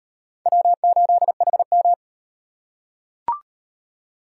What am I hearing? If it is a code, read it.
Morse code, heard as a keyed tone, about 38 wpm, 700 Hz. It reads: W8HM